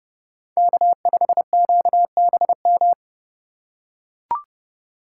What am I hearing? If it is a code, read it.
Morse code, heard as a keyed tone, about 29 wpm, 700 Hz. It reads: K5QBM
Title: USA Call Signs - 5 Characters - 2x Character Spacing - 30wpm